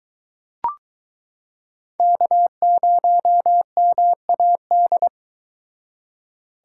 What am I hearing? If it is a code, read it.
Morse code, heard as a keyed tone, about 23 wpm, 700 Hz. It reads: K0MAD